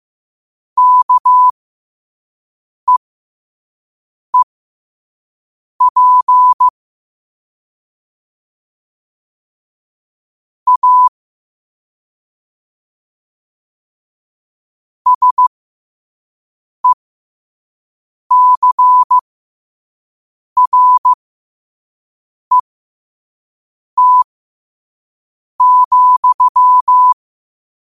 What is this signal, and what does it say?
Morse code, heard as a keyed tone, about 15 wpm, spaced as for 5 wpm, 1000 Hz. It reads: KEEP A SECRET,
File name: qotdfiles / 5wpm